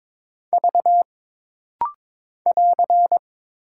Morse code, heard as a keyed tone, about 22 wpm, 700 Hz. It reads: V <AR>
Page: This rapid-fire timing is geared to push your ICR (Instant Character Recognition) skills.